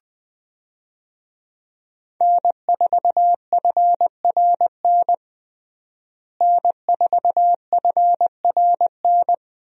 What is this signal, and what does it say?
Morse code, heard as a keyed tone, about 20 wpm, 700 Hz. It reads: N4FRN N4FRN
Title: USA Call Signs - 5 Characters - Repeated 3x - 20wpm